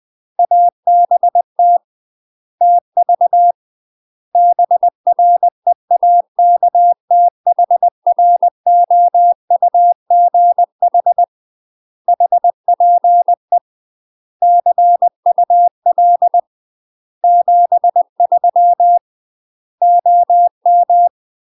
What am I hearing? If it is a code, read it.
Morse code, heard as a keyed tone, about 20 wpm, 700 Hz. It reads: ABT TV BREAKTHROUGH HPE CUL 73 OM
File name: UK MASTER QSOs TRAINING / 20WPM / UK MASTER QSOs 0067